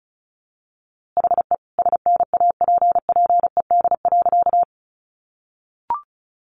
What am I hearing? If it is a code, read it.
Morse code, heard as a keyed tone, about 34 wpm, 700 Hz. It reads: HE SNAPPED.
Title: A Sound Of Thunder 35wpm